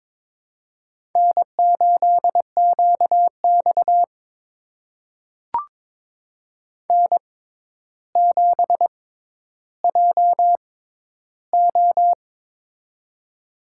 Morse code, heard as a keyed tone, about 21 wpm, spaced as for 11 wpm, 700 Hz. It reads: N8QX N7JO
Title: USA Call Signs - 4 Characters - 6x Character Spacing - 22wpm